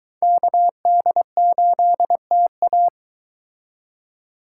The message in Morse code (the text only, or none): KD8TA